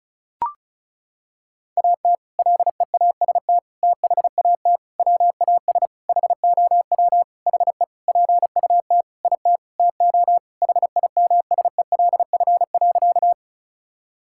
Morse code, heard as a keyed tone, about 34 wpm, 700 Hz. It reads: AT LEAST THAT WAS HOW HE PUT IT TO HIMSELF.
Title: Sentences from Top 1600 Words 35wpm